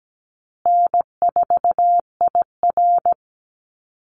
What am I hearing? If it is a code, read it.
Morse code, heard as a keyed tone, about 17 wpm, 700 Hz. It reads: N4IR
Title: Call Signs Repeated 3x - 17wpm